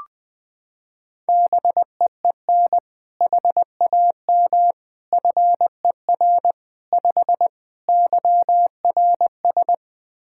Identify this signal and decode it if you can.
Morse code, heard as a keyed tone, about 21 wpm, 700 Hz. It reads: BEEN HAM FER 5 YRS
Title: K1USN SST QSO Practice 20wpm